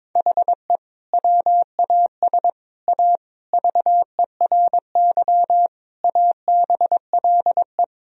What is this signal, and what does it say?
Morse code, heard as a keyed tone, about 22 wpm, 700 Hz. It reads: HE WAS A VERY ABLE
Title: Sentences from Top 1000 Words - Review Part 2 - 22wpm